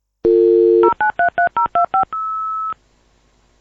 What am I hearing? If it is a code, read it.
DTMF, heard as telephone touch tones, tones of about 120 ms. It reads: 0933025
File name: Modem Dial